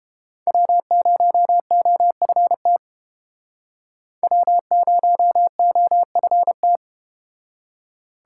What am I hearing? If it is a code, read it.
Morse code, heard as a keyed tone, about 32 wpm, 700 Hz. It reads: W0OFT W0OFT